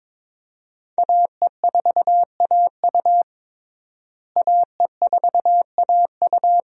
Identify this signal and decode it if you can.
Morse code, heard as a keyed tone, about 22 wpm, 700 Hz. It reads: AE4AU AE4AU
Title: USA Call Signs - 5 Characters - Repeated 2x - 22wpm